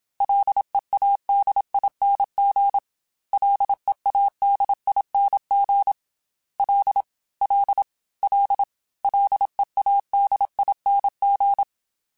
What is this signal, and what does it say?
Morse code, heard as a keyed tone, about 27 wpm, 800 Hz. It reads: LEADING LEADING L L L LEADING
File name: Ls / Single L m3s / Single L Prefixe 3x2 Twice @25